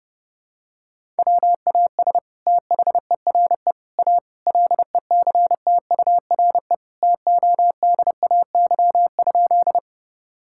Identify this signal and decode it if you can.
Morse code, heard as a keyed tone, about 30 wpm, 700 Hz. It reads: WAS THERE A LECTURE TODAY?
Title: Sentences from Top 1600 Words 30wpm